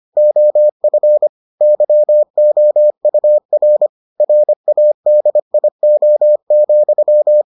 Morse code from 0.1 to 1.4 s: OF

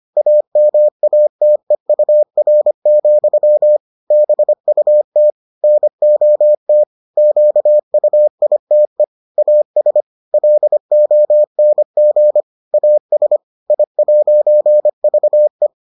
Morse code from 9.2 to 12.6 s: AS LONG

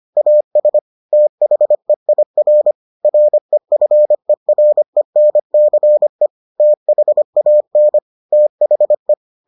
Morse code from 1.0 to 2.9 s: THEIR

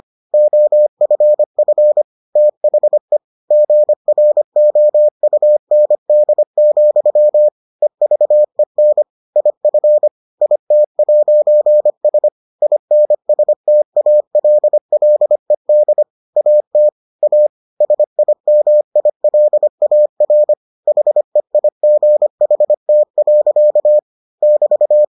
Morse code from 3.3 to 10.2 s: GROUND, EVEN IF

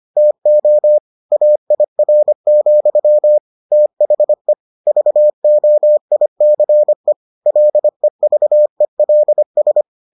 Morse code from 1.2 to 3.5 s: AIR,

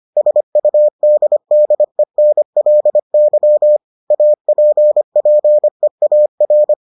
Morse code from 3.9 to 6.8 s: APPEAR

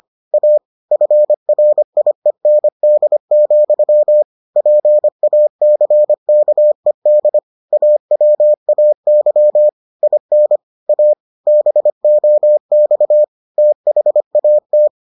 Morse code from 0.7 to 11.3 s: FRIEND, PACKED AWAY IN A

